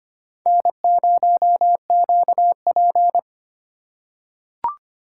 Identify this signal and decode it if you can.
Morse code, heard as a keyed tone, about 25 wpm, 700 Hz. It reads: N0QP
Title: USA Call Signs - 3x Character Spacing - 25wpm